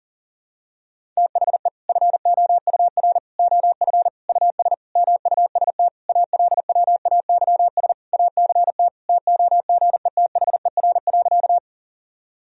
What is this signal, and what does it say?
Morse code, heard as a keyed tone, about 40 wpm, 700 Hz. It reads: THE FOUR OF US MUST ALWAYS ACT TOGETHER.
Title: Sentences from Top 1000 Words - Review Part 1 - 40wpm